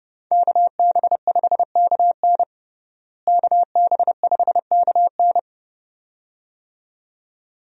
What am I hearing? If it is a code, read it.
Morse code, heard as a keyed tone, about 30 wpm, 700 Hz. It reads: KB5KN KB5KN
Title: USA Call Signs - 5 Characters - Repeated 2x - 30wpm